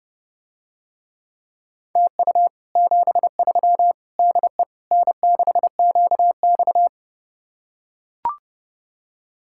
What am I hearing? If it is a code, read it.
Morse code, heard as a keyed tone, about 29 wpm, 700 Hz. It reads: TU 73 DE N6QX